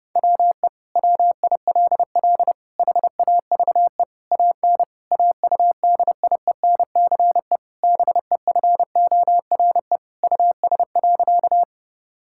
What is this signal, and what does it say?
Morse code, heard as a keyed tone, about 32 wpm, 700 Hz. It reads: WE WILL HAVE AN AUDIENCE BEFORE US.
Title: Sentences from Top 1000 Words - Review Part 5 - 30wpm